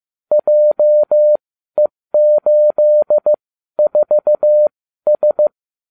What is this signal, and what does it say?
Morse code, heard as a keyed tone, about 15 wpm, spaced as for 12 wpm, 600 Hz. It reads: JE84S